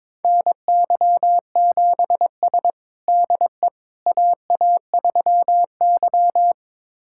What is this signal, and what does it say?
Morse code, heard as a keyed tone, about 22 wpm, 700 Hz. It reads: NY7S DE AA3Y
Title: USA Callsign DE Callsign - 4 Characters - 22wpm